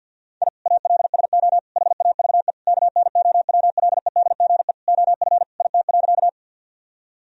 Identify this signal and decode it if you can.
Morse code, heard as a keyed tone, about 52 wpm, 700 Hz. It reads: I ALSO HAVE KNOWLEDGE OF IT.
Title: Sentences from Top 1000 Words - Review Part 3 - 50wpm